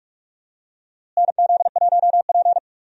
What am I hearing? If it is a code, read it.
Morse code, heard as a keyed tone, about 43 wpm, 700 Hz. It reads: NZ1P
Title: USA Call Signs - 4 Characters - Repeated 2x - 45wpm